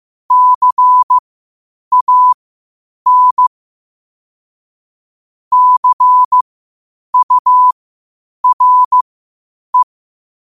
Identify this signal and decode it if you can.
Morse code, heard as a keyed tone, about 15 wpm, spaced as for 8 wpm, 1000 Hz. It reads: CAN CURE
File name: qotdfiles / 8wpm